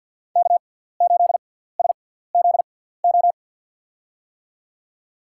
Morse code, heard as a keyed tone, about 47 wpm, spaced as for 17 wpm, 700 Hz. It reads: K8SZO
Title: USA Call Signs - 5 Characters - 6x Character Spacing - 50wpm